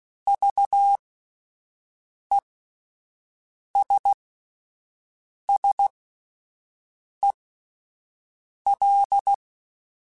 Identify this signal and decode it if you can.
Morse code, heard as a keyed tone, about 16 wpm, spaced as for 5 wpm, 800 Hz. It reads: VESSEL